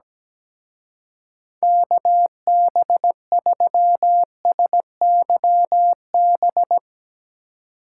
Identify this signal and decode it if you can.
Morse code, heard as a keyed tone, about 17 wpm, 700 Hz. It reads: KB3SYB
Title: Speed Racing - USA Call Signs 17wpm